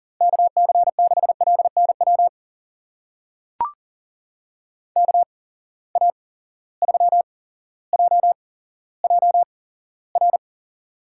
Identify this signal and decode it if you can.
Morse code, heard as a keyed tone, about 40 wpm, spaced as for 17 wpm, 700 Hz. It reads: KC6LNW KA3JJR
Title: USA Call Signs - 6 Characters - 8x Character Spacing - 40wpm